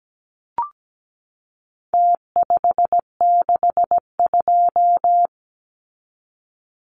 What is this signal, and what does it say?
Morse code, heard as a keyed tone, about 17 wpm, 700 Hz. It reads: T562
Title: Numbers - Contest Style - 4-Digit 17wpm